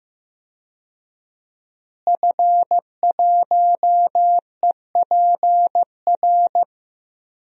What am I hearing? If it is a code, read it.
Morse code, heard as a keyed tone, about 15 wpm, 700 Hz. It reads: F1EPR
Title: DXCC-100 Call Signs - 5 Characters - Repeated 3x - 15wpm